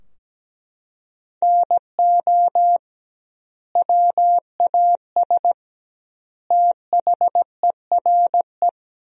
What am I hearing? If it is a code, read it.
Morse code, heard as a keyed tone, about 17 wpm, 700 Hz. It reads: NO WAS THERE
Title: 2x Word-Spacing - Sentences from Top 100 Words - 17wpm